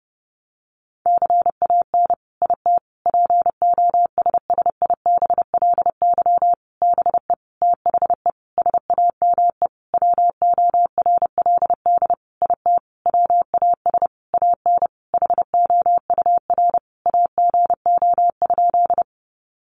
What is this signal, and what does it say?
Morse code, heard as a keyed tone, about 30 wpm, 700 Hz. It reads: CAN IT POSSIBLY BE THE SAME WORLD IT WAS AN HOUR AGO?
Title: Sentences from Top 1300 Words 30wpm